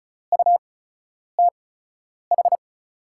Morse code, heard as a keyed tone, about 36 wpm, spaced as for 10 wpm, 700 Hz. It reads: UTH
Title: 8x Character Spacing - Top 500 Words - 35wpm